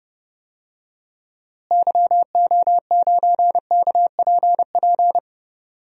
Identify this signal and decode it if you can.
Morse code, heard as a keyed tone, about 32 wpm, 700 Hz. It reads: YO9KPP
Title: DXCC-100 Call Signs - 6 Characters - Repeated 2x - 30wpm